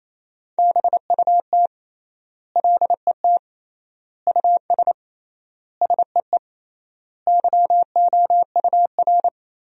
Morse code, heard as a keyed tone, about 28 wpm, 700 Hz. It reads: BUT LET US SEE YOUR
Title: 3x Word-Spacing - Sentences from Top 500 Words - 28wpm